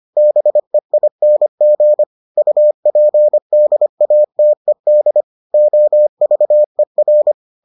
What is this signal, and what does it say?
Morse code, heard as a keyed tone, about 25 wpm, 600 Hz. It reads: BEING UPDATED OVER